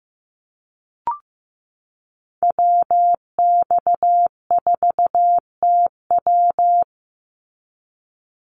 Morse code, heard as a keyed tone, about 15 wpm, 700 Hz. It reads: WX4TW
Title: NA SOTA Activator Callsigns 15wpm